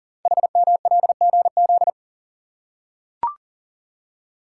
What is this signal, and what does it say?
Morse code, heard as a keyed tone, about 38 wpm, 690 Hz. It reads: HMLGZ
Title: Code Groups - 5 Letters - 40wpm